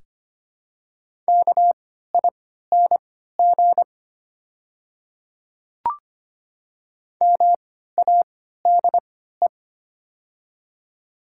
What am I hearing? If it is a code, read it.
Morse code, heard as a keyed tone, about 25 wpm, spaced as for 14 wpm, 690 Hz. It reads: KING MADE
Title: 3x Character Spacing - Top 500 Words - 25wpm